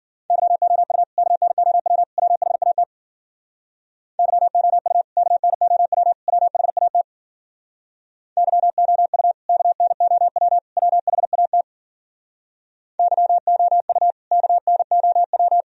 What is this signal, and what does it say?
Morse code, heard as a keyed tone, about 52 wpm, 700 Hz. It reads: YOU KNOW WHAT YOU KNOW WHAT YOU KNOW WHAT YOU KNOW